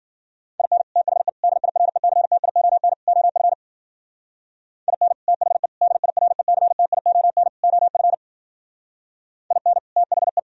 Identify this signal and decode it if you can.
Morse code, heard as a keyed tone, about 49 wpm, 700 Hz. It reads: IN THE DIRECTION OF IN THE DIRECTION OF IN THE
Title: Speed-Racing - Sets of 4 Words 40wpm